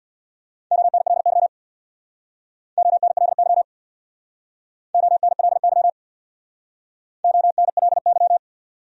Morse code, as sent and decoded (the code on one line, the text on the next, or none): --- -. .-.. -.-- / --- -. .-.. -.-- / --- -. .-.. -.-- / --- -. .-.. -.--
ONLY ONLY ONLY ONLY